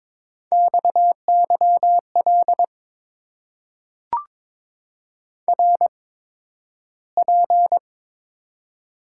Morse code, heard as a keyed tone, about 22 wpm, spaced as for 12 wpm, 700 Hz. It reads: XYL RP
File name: Top 90 QSO Elements - 8x Character Spacing - 22wpm